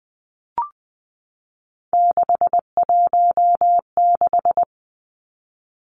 Morse code, heard as a keyed tone, about 20 wpm, 700 Hz. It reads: 616
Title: Three Challenging Letter-Number Combinations 20wpm